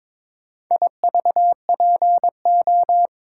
Morse code, heard as a keyed tone, about 22 wpm, 700 Hz. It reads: IVPO
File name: practice / 22wpm / Four Letter-Number Combinations - 22wpm